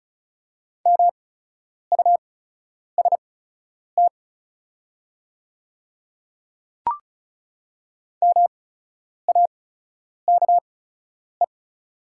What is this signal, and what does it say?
Morse code, heard as a keyed tone, about 36 wpm, spaced as for 10 wpm, 690 Hz. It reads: MUST MAKE
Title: 8x Character Spacing - Top 200 Words - 35wpm